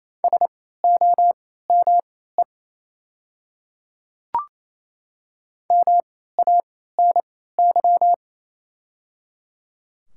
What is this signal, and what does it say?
Morse code, heard as a keyed tone, about 28 wpm, spaced as for 16 wpm, 690 Hz. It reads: SOME MANY
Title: 3x Character Spacing - Top 200 Words - 28wpm